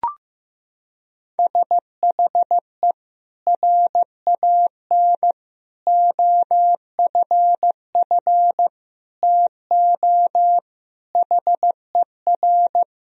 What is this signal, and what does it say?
Morse code, heard as a keyed tone, about 15 wpm, 700 Hz. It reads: SHE RAN OFF TO HER
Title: Sentences from Top 1000 Words - Review Part 3 - 15wpm